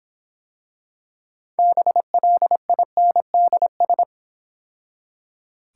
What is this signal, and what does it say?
Morse code, heard as a keyed tone, about 26 wpm, 700 Hz. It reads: BLINDS